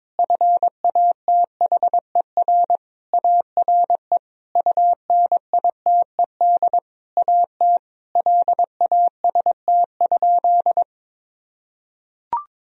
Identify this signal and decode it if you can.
Morse code, heard as a keyed tone, about 22 wpm, 700 Hz. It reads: FATHER ARE UNITED AT LAST?
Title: Sentences from Top 1000 Words - Review Part 4 - 22wpm